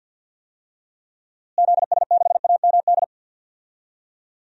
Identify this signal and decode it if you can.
Morse code, heard as a keyed tone, about 48 wpm, 700 Hz. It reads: ZS6AMD
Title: DXCC-100 Call Signs - 6 Characters - Repeated 2x - 50wpm